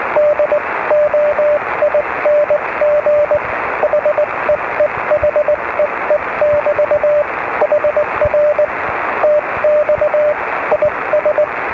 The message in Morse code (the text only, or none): DOING HEE HEE<BT> HR TX IS